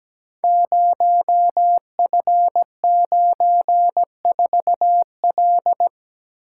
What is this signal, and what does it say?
Morse code, heard as a keyed tone, about 17 wpm, 700 Hz. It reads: 0F94L
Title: Code Groups - 5 Letter:Numbers - 17wpm